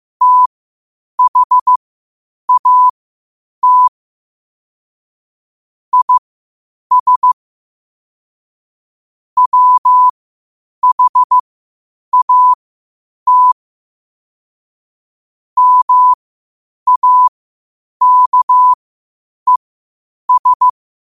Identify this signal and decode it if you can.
Morse code, heard as a keyed tone, about 15 wpm, spaced as for 8 wpm, 1000 Hz. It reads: THAT IS WHAT MAKES
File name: qotdfiles / 8wpm